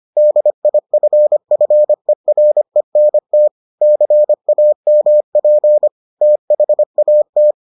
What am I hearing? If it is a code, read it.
Morse code, heard as a keyed tone, about 25 wpm, 600 Hz. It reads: DIFFERENT CAMP THAT